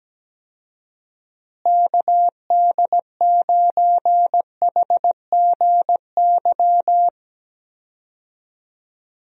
Morse code, heard as a keyed tone, about 17 wpm, 700 Hz. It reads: KD9HGY
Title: USA Call Signs - 6 Characters - Repeated 2x - 17wpm